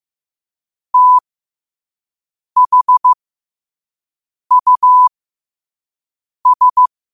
Morse code, heard as a keyed tone, about 15 wpm, spaced as for 5 wpm, 1000 Hz. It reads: THUS
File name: qotdfiles / 5wpm